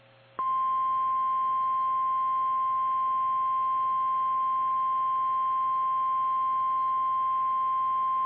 Emergency Alert System (SAME) tones (the 1050 Hz attention tone). It is heard from 0.5 to 7.2 s.
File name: eas / KKPRKXPN / KKPRKXPN